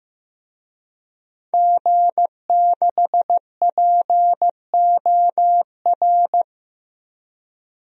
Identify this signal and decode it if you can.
Morse code, heard as a keyed tone, about 15 wpm, 700 Hz. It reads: G6POR
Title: DXCC-100 Call Signs - 5 Characters - 15wpm